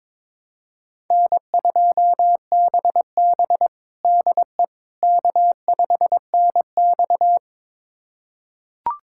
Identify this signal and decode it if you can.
Morse code, heard as a keyed tone, about 23 wpm, 700 Hz. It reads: N2BB DE K5NX
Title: USA Callsign DE Callsign - 4 Characters - 22wpm